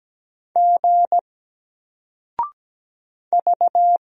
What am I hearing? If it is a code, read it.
Morse code, heard as a keyed tone, about 17 wpm, 700 Hz. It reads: G V
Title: Single Character (Letter:Number:Punctuation:Prosign) 17wpm